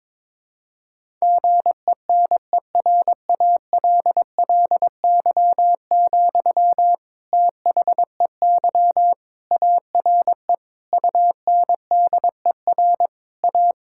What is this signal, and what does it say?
Morse code, heard as a keyed tone, about 22 wpm, 700 Hz. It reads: GENERALLY, THEY ARE UNDER A
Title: Sentences from Top 1000 Words - Review Part 6 - 22wpm